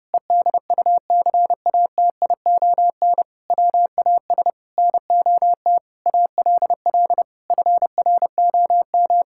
Morse code, heard as a keyed tone, about 32 wpm, 700 Hz. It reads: EDUCATION WAS NOT ALL FROM
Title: Sentences from Top 1000 Words - Review Part 6 - 30wpm